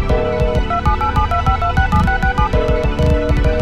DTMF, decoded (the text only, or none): 278725267667